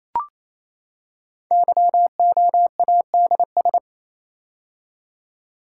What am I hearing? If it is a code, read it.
Morse code, heard as a keyed tone, about 27 wpm, 700 Hz. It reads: YOADS
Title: Code Groups - 5 Letter:Numbers - 28wpm